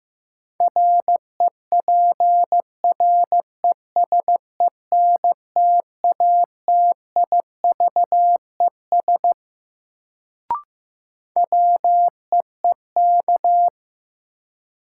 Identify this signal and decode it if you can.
Morse code, heard as a keyed tone, about 15 wpm, 700 Hz. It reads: REPRESENTATIVES WEEK
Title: Sets of 5 Words Encore - New Word Drill - 15wpm